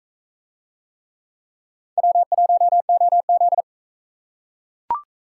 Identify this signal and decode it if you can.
Morse code, heard as a keyed tone, about 42 wpm, 700 Hz. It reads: W1OZ